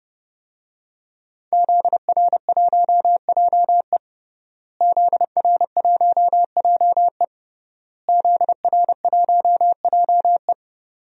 Morse code, heard as a keyed tone, about 30 wpm, 700 Hz. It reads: ZR1JE ZR1JE ZR1JE
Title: DXCC-100 Call Signs - 5 Characters - Repeated 3x - 30wpm